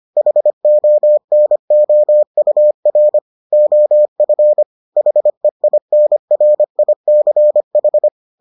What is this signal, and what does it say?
Morse code, heard as a keyed tone, about 25 wpm, 600 Hz. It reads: HONOUR OF HEINRICH